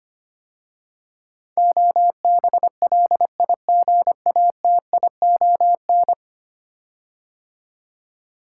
Morse code, heard as a keyed tone, about 25 wpm, 690 Hz. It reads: OBLIGATION
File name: 10 Letter Words - First 3 Letters Given - 25wpm